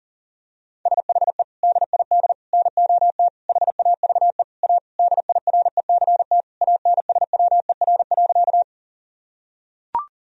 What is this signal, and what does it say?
Morse code, heard as a keyed tone, about 40 wpm, 700 Hz. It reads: SHE DID NOT HAVE A DIRECT ANSWER.
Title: Sentences from Top 1000 Words - Review Part 2 - 40wpm